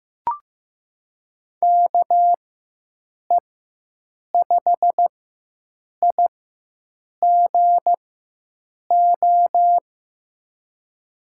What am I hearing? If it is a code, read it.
Morse code, heard as a keyed tone, about 15 wpm, spaced as for 7 wpm, 700 Hz. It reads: KE5IGO
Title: USA Call Signs - 6 Characters - 4x Character Spacing - 15wpm